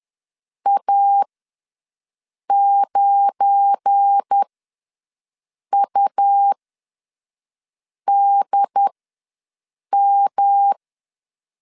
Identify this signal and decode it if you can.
Morse code, heard as a keyed tone, about 10 wpm, spaced as for 5 wpm, 800 Hz. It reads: A9UDM
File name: LEZIONI / 9 2 ws 10wpm 4130